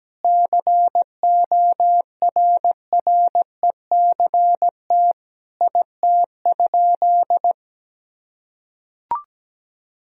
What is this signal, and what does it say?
Morse code, heard as a keyed tone, about 17 wpm, 700 Hz. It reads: CORRECT IT?
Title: Sentences from Top 1000 Words - Review Part 2 - 17wpm